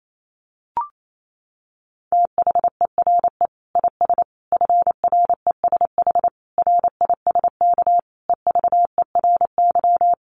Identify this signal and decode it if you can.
Morse code, heard as a keyed tone, about 28 wpm, 700 Hz. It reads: THERE IS FRESH RISK EVERY
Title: Sentences from Top 1200 Words 28wpm